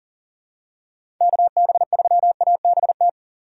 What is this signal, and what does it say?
Morse code, heard as a keyed tone, about 39 wpm, 700 Hz. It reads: KB3ABT
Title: USA Call Signs - 3x Character Spacing - 40wpm